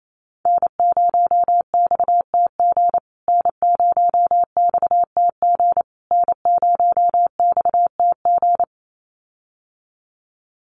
Morse code, heard as a keyed tone, about 28 wpm, 700 Hz. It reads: N0XTG N0XTG N0XTG
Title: USA Call Signs Repeated 3x 28wpm